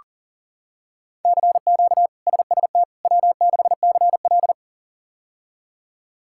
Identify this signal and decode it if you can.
Morse code, heard as a keyed tone, about 37 wpm, 700 Hz. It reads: CQ SST W6CL
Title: K1USN SST QSO Practice - 40wpm